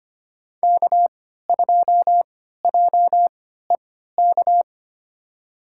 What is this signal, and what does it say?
Morse code, heard as a keyed tone, about 25 wpm, spaced as for 14 wpm, 700 Hz. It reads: K2JEK
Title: USA Call Signs - 3x Character Spacing - 25wpm